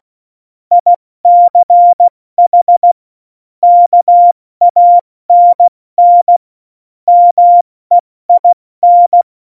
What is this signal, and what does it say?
Morse code, heard as a keyed tone, about 16 wpm, 700 Hz. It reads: ICH KANN MEIN
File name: Morsekurs / Witze / Witz 26. November 2019, 70 F80